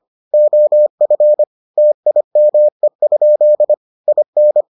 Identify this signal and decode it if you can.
Morse code, heard as a keyed tone, about 25 wpm, 600 Hz. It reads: OF TIME? IN